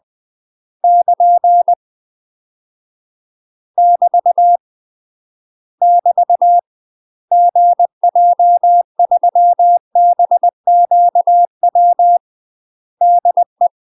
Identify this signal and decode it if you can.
Morse code, heard as a keyed tone, about 20 wpm, 700 Hz. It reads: <KN> <BT> <BT> GJ3BQW DE
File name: UK MASTER QSOs TRAINING / 20WPM / UK MASTER QSOs 0062